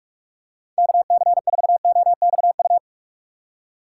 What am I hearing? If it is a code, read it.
Morse code, heard as a keyed tone, about 46 wpm, 700 Hz. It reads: KC4OXU
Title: USA Call Signs - 6 Characters - 4x Character Spacing - 45wpm